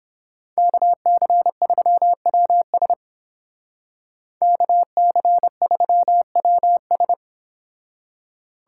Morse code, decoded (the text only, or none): KC3WS KC3WS